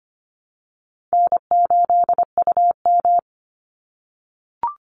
Morse code, heard as a keyed tone, about 25 wpm, 700 Hz. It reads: N8UM
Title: Call Signs - 25wpm